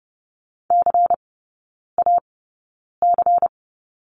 Morse code, heard as a keyed tone, about 30 wpm, spaced as for 9 wpm, 700 Hz. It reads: CAC
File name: Lesson 16e - Letter c - Three Characters 30wpm